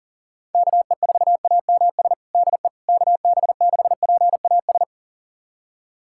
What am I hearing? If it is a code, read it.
Morse code, heard as a keyed tone, about 40 wpm, 700 Hz. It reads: KE4AMS DE KB6PAS
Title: USA Callsign DE Callsign 40wpm